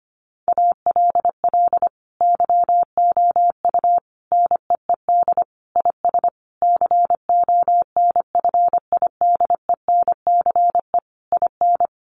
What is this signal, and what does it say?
Morse code, heard as a keyed tone, about 25 wpm, 700 Hz. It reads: ALL YOU NEED IS CONFIDENCE IN
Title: Sentences from Top 1500 Words 25wpm